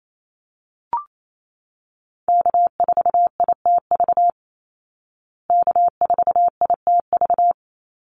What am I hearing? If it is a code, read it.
Morse code, heard as a keyed tone, about 28 wpm, 700 Hz. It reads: K4ITV K4ITV
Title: Call Signs Repeated 3x - 28wpm